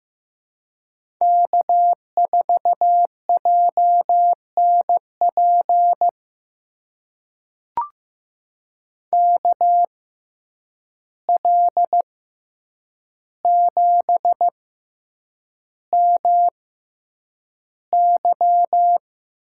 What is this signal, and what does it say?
Morse code, heard as a keyed tone, about 15 wpm, spaced as for 8 wpm, 700 Hz. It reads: K4JNP KL7MY
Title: USA Call Signs - 5 Characters - 6x Character Spacing - 15wpm